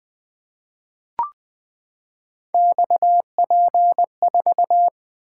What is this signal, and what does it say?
Morse code, heard as a keyed tone, about 20 wpm, 700 Hz. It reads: XP4
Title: Three Letter-Number Combinations - 20wpm